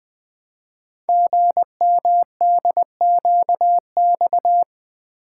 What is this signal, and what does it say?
Morse code, heard as a keyed tone, about 20 wpm, 700 Hz. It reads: GMDQX
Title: Code Groups - 5 Letters - 20wpm